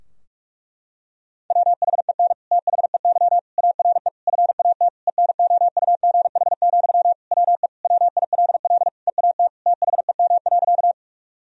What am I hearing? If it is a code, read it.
Morse code, heard as a keyed tone, about 46 wpm, 700 Hz. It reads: WHEN THEY ARE FAT ENOUGH, WE WILL EAT THEM.
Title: Sentences from Top 1000 Words - Review Part 4 - 45wpm